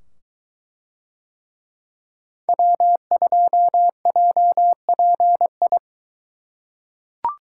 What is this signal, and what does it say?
Morse code, heard as a keyed tone, about 22 wpm, 700 Hz. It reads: W2JPI